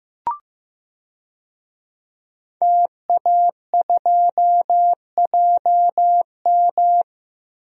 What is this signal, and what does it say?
Morse code, heard as a keyed tone, about 15 wpm, 700 Hz. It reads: TA2JM
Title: DXCC-100 Call Signs - 5 Characters - 15wpm